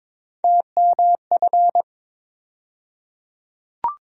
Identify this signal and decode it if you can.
Morse code, heard as a keyed tone, about 22 wpm, 700 Hz. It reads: TMF
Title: USA Call Sign Postfixes - 3 Letters 22wpm